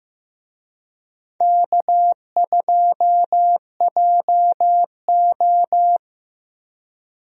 Morse code, heard as a keyed tone, about 16 wpm, 700 Hz. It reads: K2JO
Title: USA Call Signs - 4 Characters - Repeated 2x - 15wpm